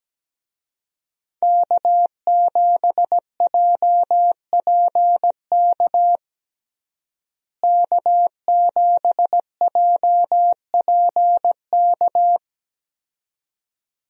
Morse code, heard as a keyed tone, about 17 wpm, 700 Hz. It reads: K7JPK K7JPK
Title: USA Call Signs - 5 Characters - Repeated 2x - 17wpm